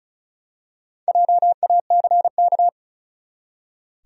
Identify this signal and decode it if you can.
Morse code, heard as a keyed tone, about 36 wpm, 700 Hz. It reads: JACK
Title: USA Common Names - Max 5 Letters - 35wpm